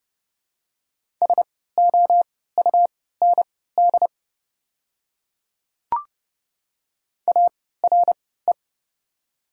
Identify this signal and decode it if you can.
Morse code, heard as a keyed tone, about 30 wpm, spaced as for 17 wpm, 700 Hz. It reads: SOUND ARE
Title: 3x Character Spacing - Top 100 Words - 30wpm